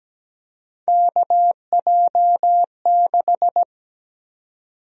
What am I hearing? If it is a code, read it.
Morse code, heard as a keyed tone, about 17 wpm, 700 Hz. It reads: KJ6
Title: USA Call Sign Prefixes - 17wpm